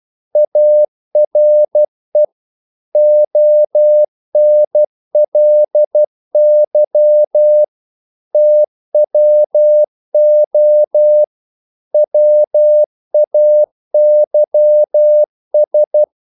Morse code, heard as a keyed tone, about 12 wpm, 600 Hz. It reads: ARE ONLY TWO WAYS
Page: English quotations in CW (telegraphy - morse)